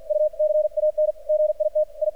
Morse code, heard as a keyed tone, about 46 wpm, 610 Hz. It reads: U CAN GIT I
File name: archived / CRSnet